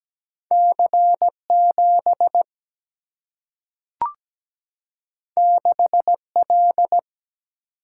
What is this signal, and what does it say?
Morse code, heard as a keyed tone, about 17 wpm, 700 Hz. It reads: C7 6L
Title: Single-Letter Single-Number 17wpm